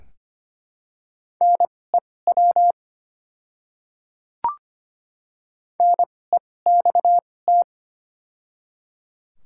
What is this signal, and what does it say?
Morse code, heard as a keyed tone, about 25 wpm, spaced as for 18 wpm, 700 Hz. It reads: NEW NEXT
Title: 2x Character Spacing - Top 300 Words - 25wpm